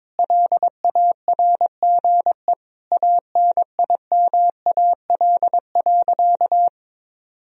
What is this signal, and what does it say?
Morse code, heard as a keyed tone, about 22 wpm, 700 Hz. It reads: LARGE ANIMAL.
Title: Sentences from Top 1000 Words - Review Part 1 - 22wpm